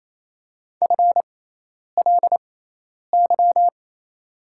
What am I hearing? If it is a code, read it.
Morse code, heard as a keyed tone, about 28 wpm, spaced as for 10 wpm, 700 Hz. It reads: FLY
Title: 6x Character Spacing - Top 500 Words - 28wpm